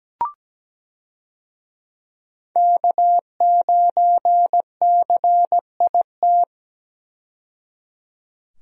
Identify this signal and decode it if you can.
Morse code, heard as a keyed tone, about 17 wpm, 700 Hz. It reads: K9CIT
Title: USA Call Signs - 5 Characters - 17wpm